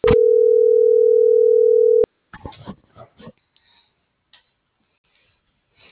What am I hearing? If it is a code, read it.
Ringback tone (North America), 440 + 480 Hz.